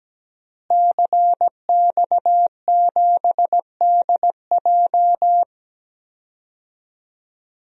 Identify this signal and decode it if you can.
Morse code, heard as a keyed tone, about 17 wpm, 700 Hz. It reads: CX7DJ
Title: DXCC-100 Call Signs - 5 Characters - Repeated 3x - 17wpm